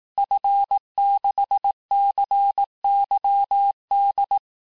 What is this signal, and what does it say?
Morse code, heard as a keyed tone, about 18 wpm, 800 Hz. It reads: F6CYD
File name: cours-du-lundi files / cours-1588-FIN-de-QSO-vitesse-17MPM